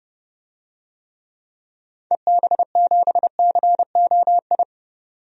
Morse code, heard as a keyed tone, about 30 wpm, 700 Hz. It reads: EB7COI
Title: DXCC-100 Call Signs - 6 Characters - 30wpm